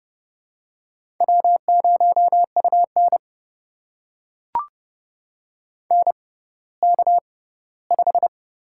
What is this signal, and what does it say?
Morse code, heard as a keyed tone, about 29 wpm, spaced as for 17 wpm, 700 Hz. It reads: W0UN NK5
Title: USA Call Signs - 4 Characters - 6x Character Spacing - 30wpm